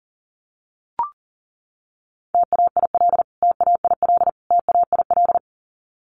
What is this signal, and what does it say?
Morse code, heard as a keyed tone, about 40 wpm, 700 Hz. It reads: TAIL TAIL TAIL
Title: Top 500 Words Repeated 40wpm